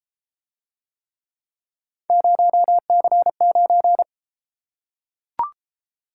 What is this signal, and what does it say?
Morse code, heard as a keyed tone, about 32 wpm, 700 Hz. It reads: 0C9